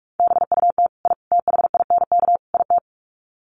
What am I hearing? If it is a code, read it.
Morse code, heard as a keyed tone, about 44 wpm, 700 Hz. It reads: BUT I THINK IT
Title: Sets of 4 Words 45wpm